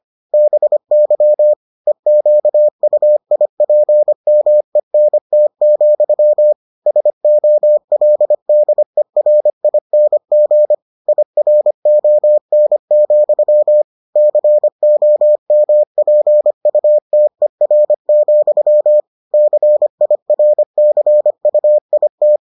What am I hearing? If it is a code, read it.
Morse code, heard as a keyed tone, about 25 wpm, 600 Hz. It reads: BY EQUIPMENT, SOLDERING IRON, COMPUTER, CIRCUIT